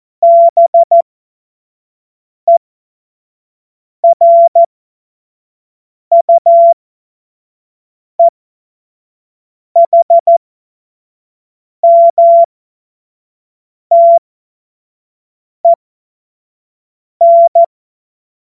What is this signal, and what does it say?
Morse code, heard as a keyed tone, about 14 wpm, spaced as for 5 wpm, 680 Hz. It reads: BERUEHMTEN
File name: Morsekurs / Witze / Witz 06. September 2022 25 F70